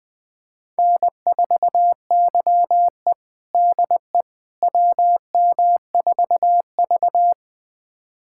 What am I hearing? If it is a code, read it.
Morse code, heard as a keyed tone, about 20 wpm, 700 Hz. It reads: N4YE DE WM4V
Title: USA Callsign DE Callsign - 4 Characters - 20wpm